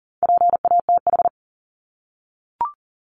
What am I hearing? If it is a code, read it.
Morse code, heard as a keyed tone, about 40 wpm, 700 Hz. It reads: PATH
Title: Top 1251-1500 Words 40wpm